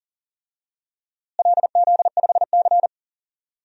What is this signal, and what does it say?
Morse code, heard as a keyed tone, about 40 wpm, 700 Hz. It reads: LZ5C
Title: DXCC-100 Call Signs - 4 Characters - Repeated 3x - 40wpm